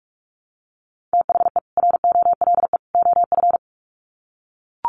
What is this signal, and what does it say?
Morse code, heard as a keyed tone, about 44 wpm, 700 Hz. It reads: THE ROLE OF
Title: Sets of 3 Words 45wpm